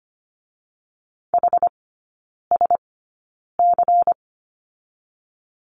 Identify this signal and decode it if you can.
Morse code, heard as a keyed tone, about 25 wpm, 700 Hz. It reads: H S C
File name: Lesson 17e - Letter u - Three Characters 25wpm